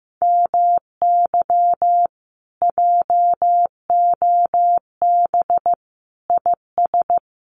Morse code, heard as a keyed tone, about 15 wpm, 700 Hz. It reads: MY JOB IS
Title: Common QSO Phrases Encore 15wpm